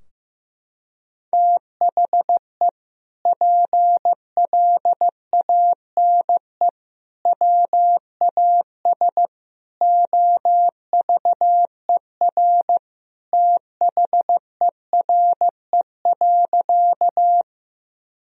Morse code, heard as a keyed tone, about 15 wpm, 700 Hz. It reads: THE PLANE WAS OVER THERE.
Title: Sentences from Top 1000 Words - Review Part 3 - 15wpm